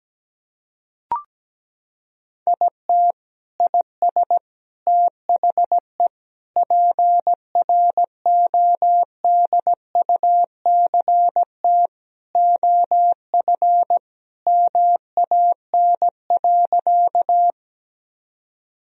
Morse code, heard as a keyed tone, about 17 wpm, 700 Hz. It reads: IT IS THE PRODUCT OF MAN.
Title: Sentences from Top 1000 Words - Review Part 2 - 17wpm